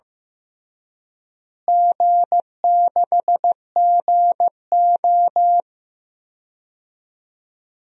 Morse code, heard as a keyed tone, about 15 wpm, 700 Hz. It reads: G6GO
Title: DXCC-100 Call Signs - 4 Characters - Repeated 2x - 15wpm